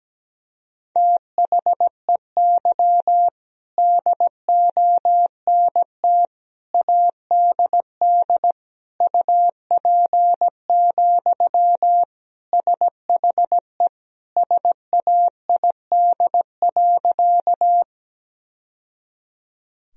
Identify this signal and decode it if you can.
Morse code, heard as a keyed tone, about 17 wpm, 700 Hz. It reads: THEY DONT ADD UP, SHE SAID.
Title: Sentences from Top 1000 Words - Review Part 1 - 17wpm